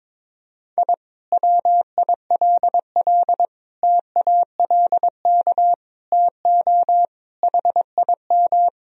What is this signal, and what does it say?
Morse code, heard as a keyed tone, about 22 wpm, 700 Hz. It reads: I WILL TALK TO HIM
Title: Sentences from Top 1000 Words - Review Part 2 - 22wpm